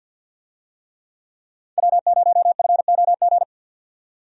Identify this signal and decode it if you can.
Morse code, heard as a keyed tone, about 50 wpm, 700 Hz. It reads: W0FOG
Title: USA Call Signs - 5 Characters - 50wpm